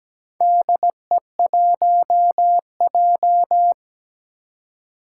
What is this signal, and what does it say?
Morse code, heard as a keyed tone, about 17 wpm, 700 Hz. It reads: DE1J
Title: Four Challenging Letter-Number Combinations 17wpm